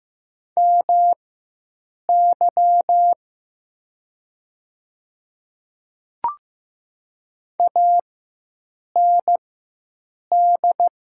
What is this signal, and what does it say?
Morse code, heard as a keyed tone, about 15 wpm, spaced as for 7 wpm, 700 Hz. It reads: MY AND
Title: 4x Character Spacing - Top 200 Words - 15wpm